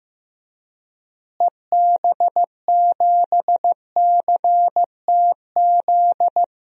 Morse code, heard as a keyed tone, about 15 wpm, 700 Hz. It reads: EB7CTZ
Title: DXCC-100 Call Signs - 6 Characters - Repeated 3x - 15wpm